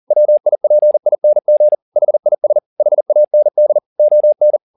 Morse code, heard as a keyed tone, about 40 wpm, 600 Hz. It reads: WIPING HIS HAND ON